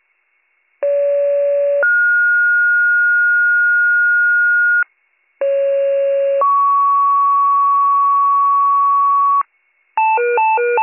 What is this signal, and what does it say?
Two-tone page, heard as a one-second tone, then a three-second tone, about 570 then 1430 Hz, about 550 then 1120 Hz, with noise. It is heard from 0.8 to 9.4 s.